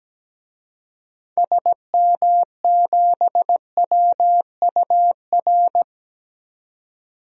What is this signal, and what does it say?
Morse code, heard as a keyed tone, about 17 wpm, 700 Hz. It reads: SM7WUR
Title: DXCC-100 Call Signs - 6 Characters - 17wpm